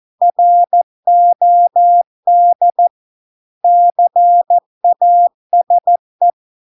Morse code, heard as a keyed tone, about 14 wpm, 700 Hz. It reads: ROD CASE